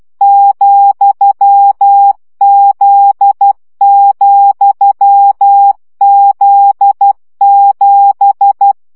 Morse code, heard as a keyed tone, about 12 wpm, 800 Hz. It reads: ,Z,Z7
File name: CW gyakorlas z 7 vesszo 10 baud